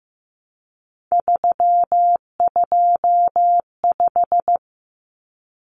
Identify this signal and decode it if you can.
Morse code, heard as a keyed tone, about 15 wpm, 700 Hz. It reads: 325
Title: Numbers - 3-Digit 15wpm